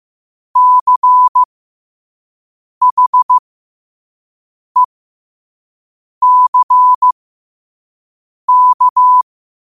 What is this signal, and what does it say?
Morse code, heard as a keyed tone, about 15 wpm, spaced as for 5 wpm, 1000 Hz. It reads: CHECK